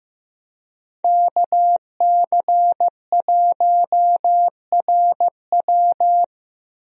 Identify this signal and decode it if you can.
Morse code, heard as a keyed tone, about 15 wpm, 700 Hz. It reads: KC1RW
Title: USA Call Signs - 5 Characters - Repeated 2x - 15wpm